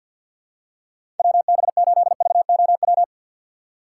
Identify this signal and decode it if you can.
Morse code, heard as a keyed tone, about 52 wpm, 700 Hz. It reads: WB8VOW
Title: USA Call Signs - 6 Characters - 3x Character Spacing - 50wpm